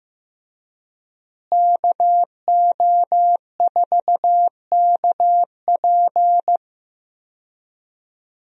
Morse code, heard as a keyed tone, about 15 wpm, 700 Hz. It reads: KO4KP
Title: USA Call Signs - 5 Characters - Repeated 2x - 15wpm